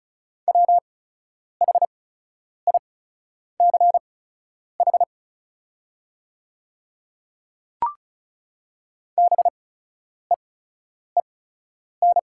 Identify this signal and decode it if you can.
Morse code, heard as a keyed tone, about 34 wpm, spaced as for 10 wpm, 700 Hz. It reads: WHICH BEEN
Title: 8x Character Spacing - Top 200 Words - 35wpm